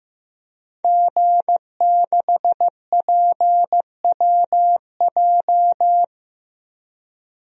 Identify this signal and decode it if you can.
Morse code, heard as a keyed tone, about 15 wpm, 700 Hz. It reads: G6PWJ
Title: DXCC-100 Call Signs - 5 Characters - Repeated 3x - 15wpm